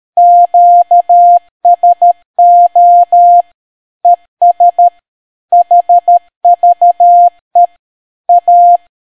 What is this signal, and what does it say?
Morse code, heard as a keyed tone, about 13 wpm, 700 Hz. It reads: QSO ES HVE A